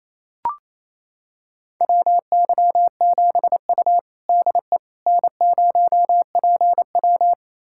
Morse code, heard as a keyed tone, about 28 wpm, 700 Hz. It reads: WY7U DE N0PW
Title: USA Callsign DE Callsign - 4 Characters - 28wpm